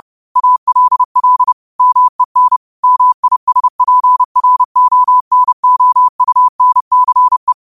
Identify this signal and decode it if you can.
Morse code, heard as a keyed tone, about 30 wpm, 1000 Hz. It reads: ALL MEN MISPRONOUNCE